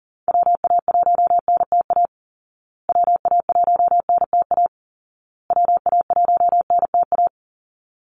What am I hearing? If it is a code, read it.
Morse code, heard as a keyed tone, about 40 wpm, 700 Hz. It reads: WA1NTA WA1NTA WA1NTA
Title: Call Signs Repeated 2x - 40wpm